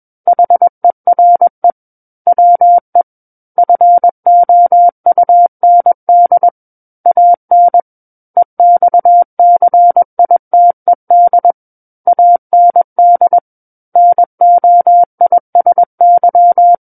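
Morse code, heard as a keyed tone, about 21 wpm, 700 Hz. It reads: HERE WE FOUND AN EXCITED AND NOISY
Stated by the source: Morse code practice reading of a book chapter at 21 words per minute.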